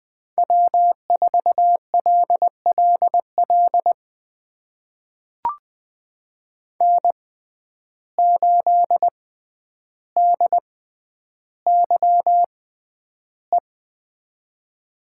Morse code, heard as a keyed tone, about 20 wpm, spaced as for 10 wpm, 700 Hz. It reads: W4LLL N8DYE
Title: USA Call Signs - 5 Characters - 6x Character Spacing - 20wpm